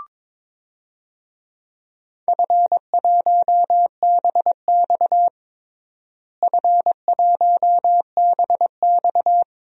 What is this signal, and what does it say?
Morse code, heard as a keyed tone, about 22 wpm, 700 Hz. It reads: F1BX F1BX
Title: DXCC-100 Call Signs - 4 Characters - Repeated 2x - 22wpm